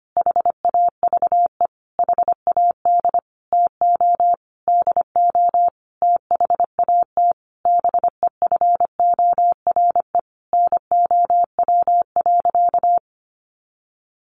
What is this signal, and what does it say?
Morse code, heard as a keyed tone, about 25 wpm, 700 Hz. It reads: HAVE HAD TO DO THAT BEFORE NOW.
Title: Sentences from Top 1300 Words 25wpm